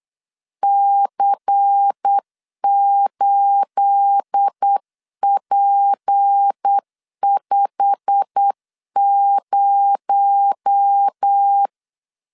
Morse code, heard as a keyed tone, about 8 wpm, 800 Hz. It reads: C8P50
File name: CW / DET / 1 9 1500C S A 8wpm 5637